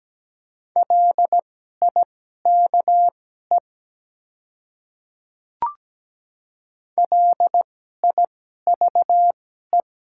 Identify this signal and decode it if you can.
Morse code, heard as a keyed tone, about 17 wpm, spaced as for 12 wpm, 700 Hz. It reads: LIKE LIVE
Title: 2x Character Spacing - Top 300 Words - 17wpm